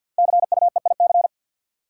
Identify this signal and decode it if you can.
Morse code, heard as a keyed tone, about 52 wpm, 700 Hz. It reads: CUEIC
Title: Code Groups - 5 Letter:Numbers - 50wpm